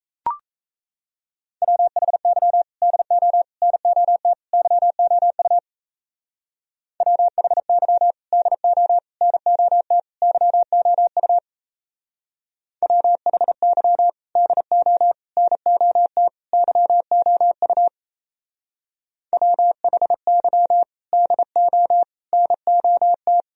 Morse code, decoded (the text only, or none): WHY DO NOT YOU WHY DO NOT YOU WHY DO NOT YOU WHY DO NOT